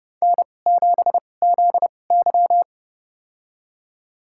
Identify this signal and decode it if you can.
Morse code, heard as a keyed tone, about 29 wpm, spaced as for 21 wpm, 700 Hz. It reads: N7ZY
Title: USA Call Signs - 2x Character Spacing - 30wpm